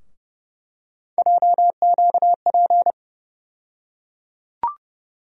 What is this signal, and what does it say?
Morse code, heard as a keyed tone, about 30 wpm, 700 Hz. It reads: JQP
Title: USA Call Sign Postfixes - 3 Letters 30wpm